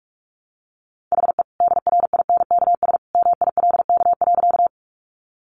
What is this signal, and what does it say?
Morse code, heard as a keyed tone, about 44 wpm, 700 Hz. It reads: HE DRINKS MILK.
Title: Easy Phrases 45wpm